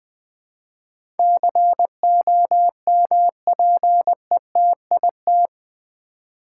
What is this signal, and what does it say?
Morse code, heard as a keyed tone, about 20 wpm, 700 Hz. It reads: COMPETIT
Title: Build A Word - 4 Letters At A Time - 11-Letter Words - 20wpm